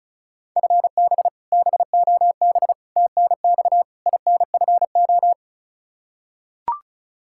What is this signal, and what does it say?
Morse code, heard as a keyed tone, about 34 wpm, 700 Hz. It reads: FB BOB TNX INFO
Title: K1USN SST QSO Practice 35wpm